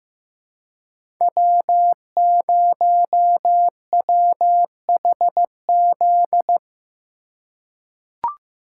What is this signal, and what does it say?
Morse code, heard as a keyed tone, about 15 wpm, 700 Hz. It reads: W0WHZ
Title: USA Call Signs - 5 Characters - 2x Character Spacing - 15wpm